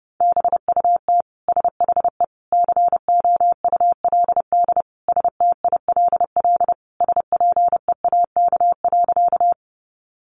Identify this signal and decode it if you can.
Morse code, heard as a keyed tone, about 30 wpm, 700 Hz. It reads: BUT SHE COULD STILL SPEAK.
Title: Sentences from Top 1100 Words 30wpm